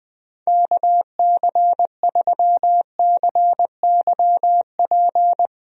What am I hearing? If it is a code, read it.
Morse code, heard as a keyed tone, about 20 wpm, 700 Hz. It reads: KC3CYP
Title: USA Call Signs - 6 Characters - Repeated 3x - 20wpm